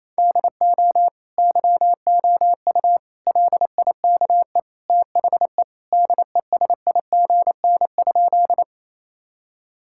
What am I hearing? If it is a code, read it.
Morse code, heard as a keyed tone, about 28 wpm, 700 Hz. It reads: DO YOU LIKE THE DESIGN?
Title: Sentences from Top 1000 Words - Review Part 5 - 28wpm